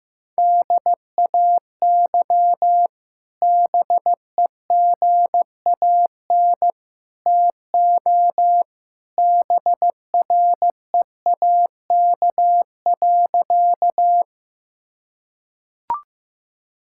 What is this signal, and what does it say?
Morse code, heard as a keyed tone, about 16 wpm, 700 Hz. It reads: DAY BEGAN TO BREAK.
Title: Sentences from Top 1000 Words - Review Part 6 - 15wpm